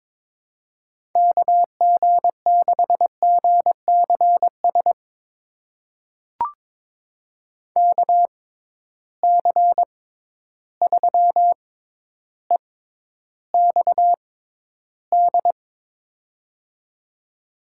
Morse code, heard as a keyed tone, about 21 wpm, spaced as for 11 wpm, 700 Hz. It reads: KG6GCS KC3EXD
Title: USA Call Signs - 6 Characters - 6x Character Spacing - 22wpm